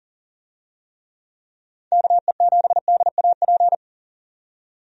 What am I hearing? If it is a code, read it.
Morse code, heard as a keyed tone, about 40 wpm, 700 Hz. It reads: KE7DAP
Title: USA Call Signs - 6 Characters - 40wpm